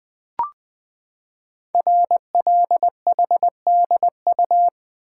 Morse code, heard as a keyed tone, about 20 wpm, 700 Hz. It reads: RLHDU
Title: Code Groups - 5 Letters - 20wpm